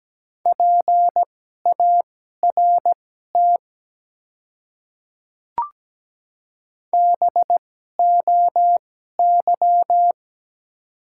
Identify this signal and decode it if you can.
Morse code, heard as a keyed tone, about 17 wpm, spaced as for 12 wpm, 700 Hz. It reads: PART BOY
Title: 2x Character Spacing - Top 300 Words - 17wpm